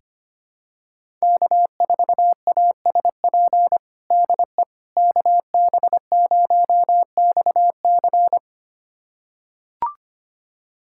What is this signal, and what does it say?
Morse code, heard as a keyed tone, about 25 wpm, 700 Hz. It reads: K4ASP DE KB0XC
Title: USA Callsign DE Callsign - 5 Characters - 25wpm